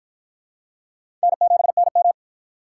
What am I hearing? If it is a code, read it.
Morse code, heard as a keyed tone, about 53 wpm, 700 Hz. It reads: N7NM